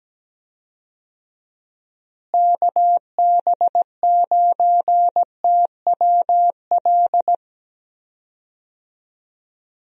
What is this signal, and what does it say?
Morse code, heard as a keyed tone, about 17 wpm, 700 Hz. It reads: KB9TWL
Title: USA Call Signs - 6 Characters - Repeated 2x - 17wpm